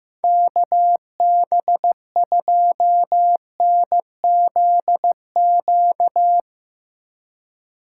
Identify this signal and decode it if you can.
Morse code, heard as a keyed tone, about 15 wpm, 700 Hz. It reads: KB2NZQ
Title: Speed Racing - USA 2x3 Call Signs 15wpm